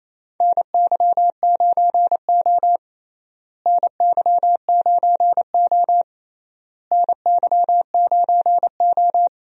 Morse code, heard as a keyed tone, about 28 wpm, 700 Hz. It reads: NY9O NY9O NY9O
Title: USA Call Signs - 4 Characters - Repeated 3x - 28wpm